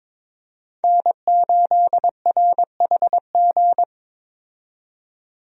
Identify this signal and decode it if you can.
Morse code, heard as a keyed tone, about 22 wpm, 700 Hz. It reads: N8RHG
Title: USA Call Signs - 5 Characters - 22wpm